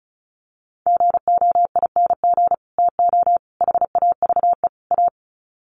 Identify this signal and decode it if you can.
Morse code, heard as a keyed tone, about 34 wpm, 700 Hz. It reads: GOING TO HAVE A
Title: Sets of 4 Words 35wpm